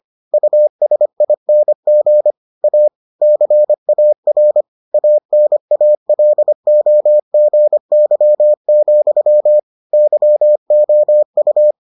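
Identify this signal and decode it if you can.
Morse code, heard as a keyed tone, about 25 wpm, 600 Hz. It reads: USING A CAR ANALOGY, YOU
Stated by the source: Morse edition of the Foundations of Amateur Radio episodes.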